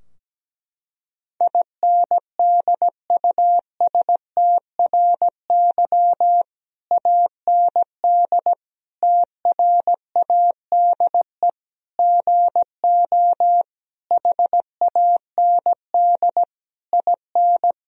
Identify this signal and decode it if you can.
Morse code, heard as a keyed tone, about 17 wpm, 700 Hz. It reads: INDUSTRY AND TRADE GO HAND IN
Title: Sentences from Top 1000 Words - Review Part 4 - 17wpm